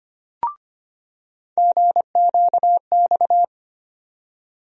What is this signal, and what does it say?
Morse code, heard as a keyed tone, about 25 wpm, 700 Hz. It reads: GQX
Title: USA Call Sign Postfixes - 3 Letters 25wpm